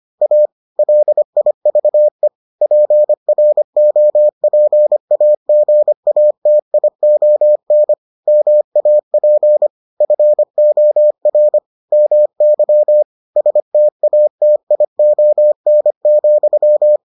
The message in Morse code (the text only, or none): A LIVE PROPAGATION MAP FOR MY STATION,